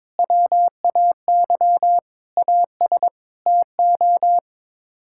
Morse code, heard as a keyed tone, about 22 wpm, 700 Hz. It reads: WAY AS TO
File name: practice / 22wpm / Sets of 6 Words - 22wpm